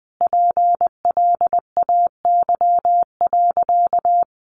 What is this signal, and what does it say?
Morse code, heard as a keyed tone, about 20 wpm, 700 Hz. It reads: PLAY.
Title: Sentences from Top 900 Words 20wpm